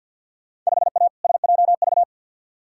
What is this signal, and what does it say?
Morse code, heard as a keyed tone, about 48 wpm, 700 Hz. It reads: 5A SJV
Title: Field Day Exchanges - 50wpm